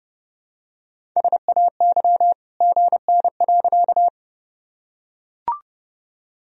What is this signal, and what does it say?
Morse code, heard as a keyed tone, about 30 wpm, 700 Hz. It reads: SAY GN.
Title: Sentences from Top 300 Words and Abbreviations 30wpm